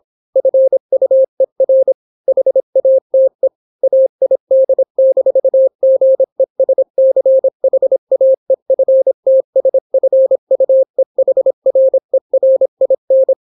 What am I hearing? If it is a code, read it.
Morse code, heard as a keyed tone, about 26 wpm, 530 Hz. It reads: FUER HATE AID-GESCHAEFTSFUEHRERIN